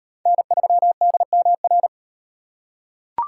Morse code, heard as a keyed tone, about 33 wpm, 700 Hz. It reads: N3DMR